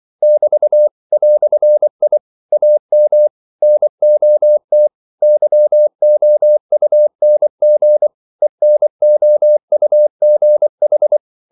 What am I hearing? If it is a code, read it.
Morse code, heard as a keyed tone, about 24 wpm, 600 Hz. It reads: <BT> "I AM NOT YOUNG ENOUGH
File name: CW / Practice / EnglishQuotations0324